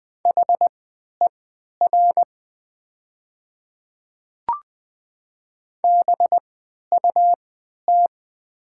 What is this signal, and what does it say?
Morse code, heard as a keyed tone, about 20 wpm, spaced as for 11 wpm, 700 Hz. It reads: HER BUT
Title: 3x Character Spacing - Top 100 Words - 20wpm